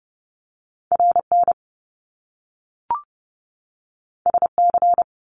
Morse code, heard as a keyed tone, about 30 wpm, 700 Hz. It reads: RN SC
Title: Top Two Letter Combinations in Words 30wpm